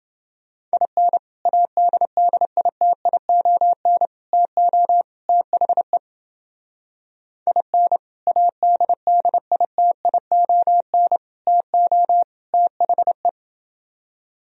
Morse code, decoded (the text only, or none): IN ADDITION TO THE IN ADDITION TO THE